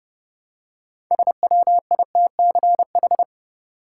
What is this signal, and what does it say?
Morse code, heard as a keyed tone, about 30 wpm, 700 Hz. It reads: SWITCH
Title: Six Letter Words - 30wpm